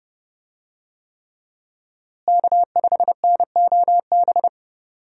Morse code, heard as a keyed tone, about 28 wpm, 700 Hz. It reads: K5NOB
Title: USA Call Signs - 5 Characters - Repeated 3x - 30wpm